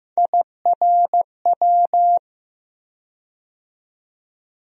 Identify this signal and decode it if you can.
Morse code, heard as a keyed tone, about 15 wpm, 700 Hz. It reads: IRW
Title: USA Call Sign Postfixes - 3 Letters 15wpm